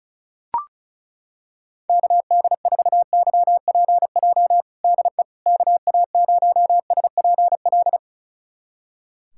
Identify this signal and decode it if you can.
Morse code, heard as a keyed tone, about 36 wpm, 700 Hz. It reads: KD4YPJ DE KA0SPL
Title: USA Callsign DE Callsign - 6 Characters - 35wpm